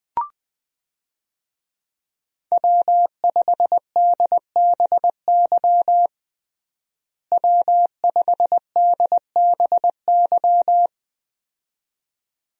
Morse code, heard as a keyed tone, about 20 wpm, 700 Hz. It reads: W5DBY W5DBY
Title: USA Call Signs - 5 Characters - Repeated 2x - 20wpm